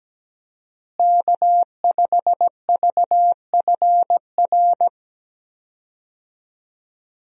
Morse code, heard as a keyed tone, about 17 wpm, 700 Hz. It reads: K5VFR
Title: USA Call Signs - 5 Characters - Repeated 2x - 17wpm